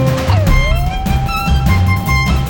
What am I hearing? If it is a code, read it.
There is no code.